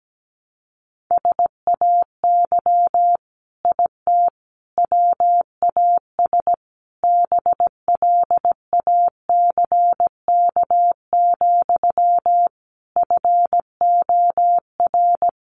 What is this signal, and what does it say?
Morse code, heard as a keyed tone, about 17 wpm, 700 Hz. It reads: SAY IT WAS BLACK, FOR
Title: Sentences from Top 800 Words 17wpm